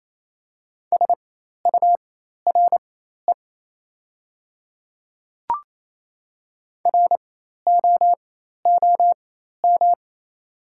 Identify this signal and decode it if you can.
Morse code, heard as a keyed tone, about 28 wpm, spaced as for 13 wpm, 700 Hz. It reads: SURE ROOM
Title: 4x Character Spacing - Top 300 Words - 28wpm